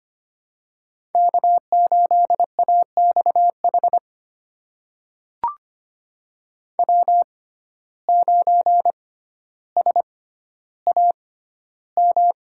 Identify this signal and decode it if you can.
Morse code, heard as a keyed tone, about 25 wpm, spaced as for 13 wpm, 700 Hz. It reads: K8AXH W9SAM
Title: USA Call Signs - 5 Characters - 6x Character Spacing - 25wpm